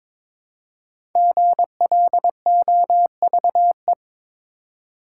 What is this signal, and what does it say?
Morse code, heard as a keyed tone, about 23 wpm, 700 Hz. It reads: GLOVE
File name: practice / 22wpm / 5 Letter Words - First 3 Letters Given - 22wpm